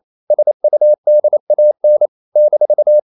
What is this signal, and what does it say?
Morse code, heard as a keyed tone, about 28 wpm, 600 Hz. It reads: SUDAN -
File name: CW / Practice / NationalCapitals28